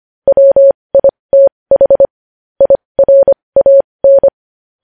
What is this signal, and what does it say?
Morse code, heard as a keyed tone, about 25 wpm, spaced as for 20 wpm, 550 Hz. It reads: WITH IRAN
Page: News Headlines in Morse Code at 20 WPM